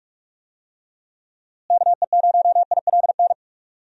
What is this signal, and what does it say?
Morse code, heard as a keyed tone, about 41 wpm, 700 Hz. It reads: KE0ILN
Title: USA Call Signs - 6 Characters - Repeated 2x - 45wpm